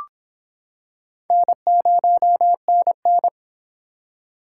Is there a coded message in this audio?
Yes: Morse code from 1.3 to 3.3 s, heard as a keyed tone, about 25 wpm, 700 Hz. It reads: N0NN